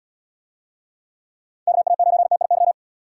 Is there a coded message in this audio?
Yes: Morse code from 1.7 to 2.7 s, heard as a keyed tone, about 75 wpm, 700 Hz. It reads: KA9TIY